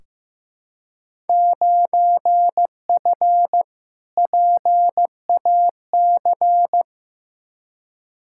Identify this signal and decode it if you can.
Morse code, heard as a keyed tone, about 16 wpm, 700 Hz. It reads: 9F PAC
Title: Field Day Exchanges Encore - 15wpm